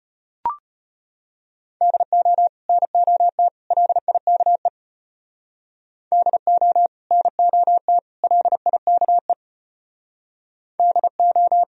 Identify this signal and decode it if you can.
Morse code, heard as a keyed tone, about 32 wpm, 700 Hz. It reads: DO NOT LIKE DO NOT LIKE DO